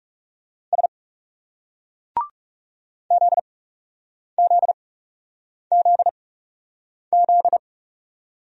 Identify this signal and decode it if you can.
Morse code, heard as a keyed tone, about 38 wpm, spaced as for 8 wpm, 700 Hz. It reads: S ZZZZ